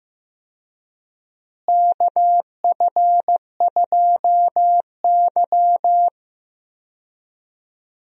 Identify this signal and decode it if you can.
Morse code, heard as a keyed tone, about 15 wpm, 700 Hz. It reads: KF2Y
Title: USA Call Signs - 4 Characters - Repeated 2x - 15wpm